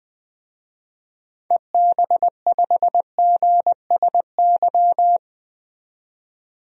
Morse code, heard as a keyed tone, about 20 wpm, 700 Hz. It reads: EB5GSY
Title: DXCC-100 Call Signs - 6 Characters - 20wpm